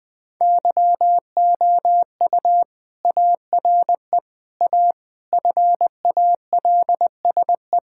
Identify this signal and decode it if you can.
Morse code, heard as a keyed tone, about 20 wpm, 700 Hz. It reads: YOU ARE A FALSE
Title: Sentences from Top 1600 Words 20wpm